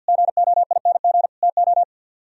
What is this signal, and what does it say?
Morse code, heard as a keyed tone, about 63 wpm, 700 Hz. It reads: GOING TO